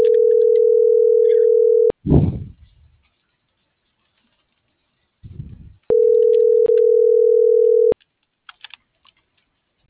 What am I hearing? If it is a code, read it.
Ringback tone (North America), 440 + 480 Hz.